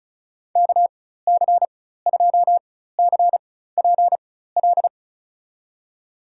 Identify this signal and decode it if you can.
Morse code, heard as a keyed tone, about 36 wpm, spaced as for 16 wpm, 700 Hz. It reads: KC2CPL
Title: USA Call Signs - 4x Character Spacing - 35wpm